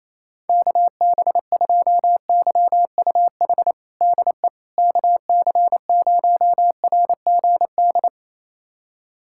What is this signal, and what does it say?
Morse code, heard as a keyed tone, about 27 wpm, 700 Hz. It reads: KB2YUH DE KC0RGD
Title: USA Callsign DE Callsign - 6 Characters - 28wpm